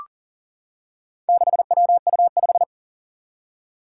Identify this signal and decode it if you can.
Morse code, heard as a keyed tone, about 40 wpm, 700 Hz. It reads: 6WU5
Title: Four Challenging Letter-Number Combinations 40wpm